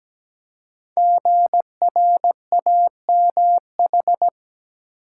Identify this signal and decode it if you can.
Morse code, heard as a keyed tone, about 17 wpm, 700 Hz. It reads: GRAMH
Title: Code Groups - 5 Letters - 17wpm